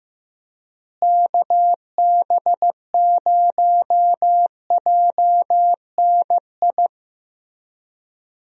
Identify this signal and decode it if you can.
Morse code, heard as a keyed tone, about 15 wpm, 700 Hz. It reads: KB0JNI
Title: USA Call Signs - 6 Characters - Repeated 2x - 15wpm